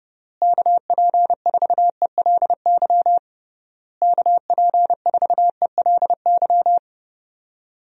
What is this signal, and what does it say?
Morse code, heard as a keyed tone, about 30 wpm, 700 Hz. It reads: KP4ELY KP4ELY
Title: USA Call Signs - 6 Characters - Repeated 3x - 30wpm